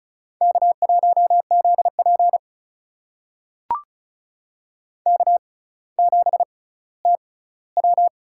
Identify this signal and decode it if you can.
Morse code, heard as a keyed tone, about 36 wpm, spaced as for 18 wpm, 700 Hz. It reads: K1ZP K7TW
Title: USA Call Signs - 4 Characters - 6x Character Spacing - 35wpm